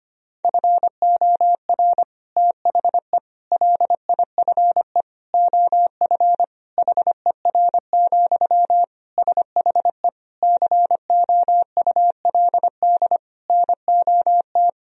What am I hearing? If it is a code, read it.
Morse code, heard as a keyed tone, about 25 wpm, 700 Hz. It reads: FOR THE LIFE OF HER, SHE COULD NOT
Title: Sentences from Top 1000 Words - Review Part 2 - 25wpm